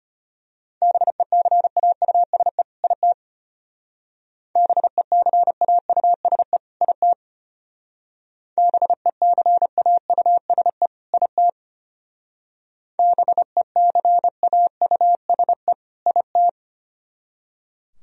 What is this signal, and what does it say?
Morse code, heard as a keyed tone, about 31 wpm, 700 Hz. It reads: BECAUSE IT BECAUSE IT BECAUSE IT BECAUSE IT
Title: Speed-Racing - Sets of 2 Words 25wpm